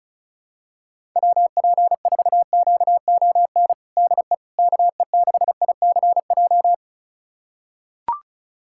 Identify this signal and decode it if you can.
Morse code, heard as a keyed tone, about 36 wpm, 700 Hz. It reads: WP4QON DE KE6ICJ
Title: USA Callsign DE Callsign 35wpm